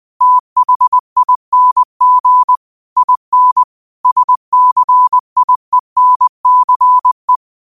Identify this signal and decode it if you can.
Morse code, heard as a keyed tone, about 20 wpm, 1000 Hz. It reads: THING IN SCIENCE